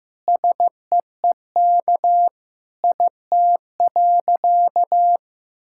Morse code, heard as a keyed tone, about 15 wpm, 700 Hz. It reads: SEEK IT.
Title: Sentences from Top 1600 Words 15wpm